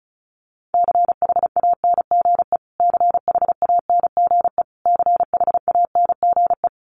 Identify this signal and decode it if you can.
Morse code, heard as a keyed tone, about 34 wpm, 700 Hz. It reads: CHANGE CHANGE CHANGE
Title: Top 300 Words Repeated 35wpm